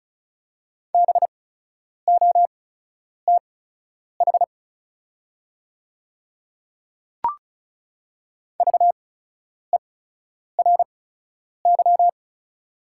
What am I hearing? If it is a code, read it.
Morse code, heard as a keyed tone, about 36 wpm, spaced as for 10 wpm, 700 Hz. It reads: BOTH VERY
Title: 8x Character Spacing - Top 500 Words - 35wpm